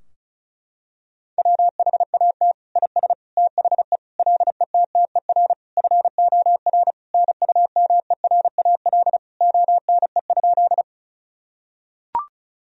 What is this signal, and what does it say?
Morse code, heard as a keyed tone, about 36 wpm, 700 Hz. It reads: WHAT IS THE LETTER FOR NUMERAL ONE?
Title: Sentences from Top 1000 Words - Review Part 2 - 35wpm